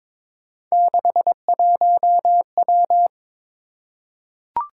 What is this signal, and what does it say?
Morse code, heard as a keyed tone, about 22 wpm, 700 Hz. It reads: 61W
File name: practice / 22wpm / Three Letter-Number Combinations - 22wpm